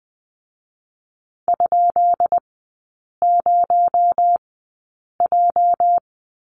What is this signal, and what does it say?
Morse code, heard as a keyed tone, about 19 wpm, spaced as for 8 wpm, 700 Hz. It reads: ?0J
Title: 218 Lesson 35e - Number 0 - Three Characters 20wpm